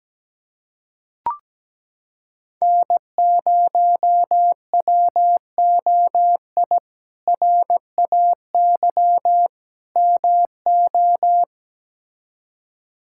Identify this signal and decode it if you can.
Morse code, heard as a keyed tone, about 17 wpm, 700 Hz. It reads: N0WOI RAY MO
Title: K1USN SST QSO Practice - 17wpm